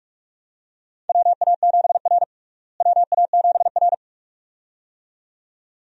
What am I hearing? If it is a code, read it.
Morse code, heard as a keyed tone, about 42 wpm, 700 Hz. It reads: WA7R WA7R
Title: USA Call Signs - 4 Characters - Repeated 2x - 45wpm